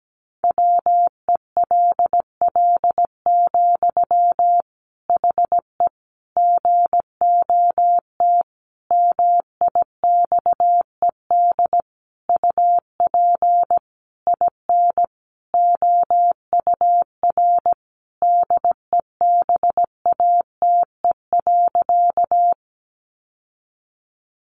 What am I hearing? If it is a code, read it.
Morse code, heard as a keyed tone, about 17 wpm, 700 Hz. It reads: WELL, HE GOT MIXED UP IN OUR DEBATE.
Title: Sentences from Top 1500 Words 17wpm